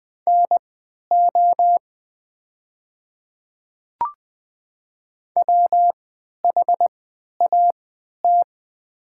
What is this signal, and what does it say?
Morse code, heard as a keyed tone, about 20 wpm, spaced as for 11 wpm, 700 Hz. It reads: NO WHAT
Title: 3x Character Spacing - Top 100 Words - 20wpm